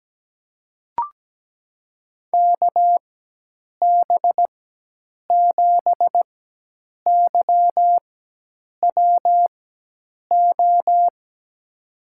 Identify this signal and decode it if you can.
Morse code, heard as a keyed tone, about 17 wpm, spaced as for 8 wpm, 700 Hz. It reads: KB7YWO
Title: USA Call Signs - 4x Character Spacing - 17wpm